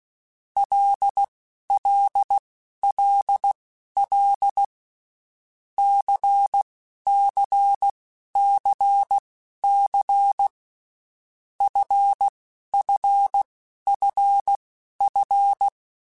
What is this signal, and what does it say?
Morse code, heard as a keyed tone, about 16 wpm, spaced as for 11 wpm, 800 Hz. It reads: LLLL CCCC FFFF
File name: Groups@10 / Groups@10 mp3s / LFCYQ@10